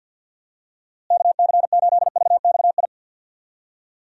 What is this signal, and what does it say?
Morse code, heard as a keyed tone, about 51 wpm, 700 Hz. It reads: KC8VXI
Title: USA Call Signs - 6 Characters - Repeated 3x - 50wpm